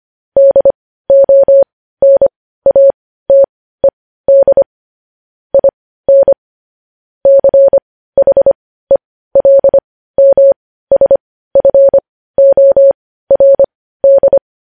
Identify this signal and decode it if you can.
Morse code, heard as a keyed tone, about 25 wpm, spaced as for 15 wpm, 550 Hz. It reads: DONATED IN CHELMSFORD
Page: news-headlines-in-morse-code-at-15-wpm